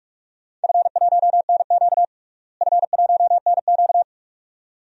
Morse code, heard as a keyed tone, about 46 wpm, 700 Hz. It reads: F1NQ F1NQ
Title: DXCC-100 Call Signs - 4 Characters - Repeated 2x - 45wpm